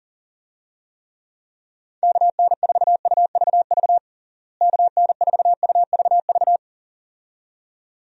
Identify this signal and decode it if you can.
Morse code, heard as a keyed tone, about 40 wpm, 700 Hz. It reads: KN4UVV KN4UVV
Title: USA Call Signs - 6 Characters - Repeated 2x - 40wpm